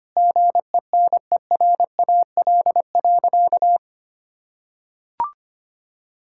Morse code, heard as a keyed tone, about 25 wpm, 700 Hz. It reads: GENERAL.
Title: Sentences from Top 1600 Words 25wpm